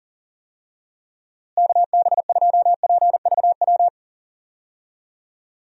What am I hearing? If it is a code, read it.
Morse code, heard as a keyed tone, about 40 wpm, 700 Hz. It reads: KB2PVW
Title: USA Call Signs - 6 Characters - Repeated 2x - 40wpm